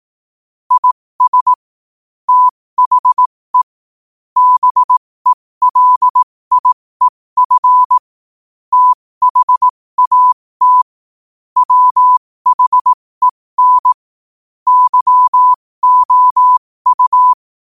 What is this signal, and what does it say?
Morse code, heard as a keyed tone, about 18 wpm, spaced as for 15 wpm, 1000 Hz. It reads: IS THE BELIEF THAT WHEN YOU
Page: qotd-at-15-wpm